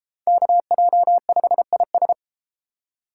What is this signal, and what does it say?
Morse code, heard as a keyed tone, about 32 wpm, 700 Hz. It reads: KJ5IS